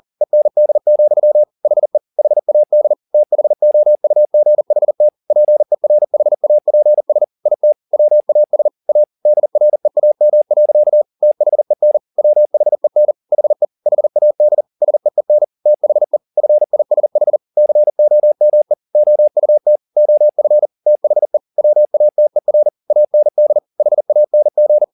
Morse code, heard as a keyed tone, about 40 wpm, 600 Hz. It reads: END, HE HAD THOUGHT PERHAPS IT WAS A DREAM. THEN WHEN HE HAD SEEN THE FISH COME OUT OF THE WATER AND HANG
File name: the old man and the sea 2 40wpm